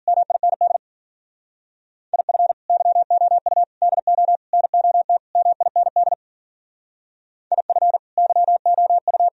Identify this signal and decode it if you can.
Morse code, heard as a keyed tone, about 52 wpm, 700 Hz. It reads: MIND IF YOU DO NOT MIND IF YOU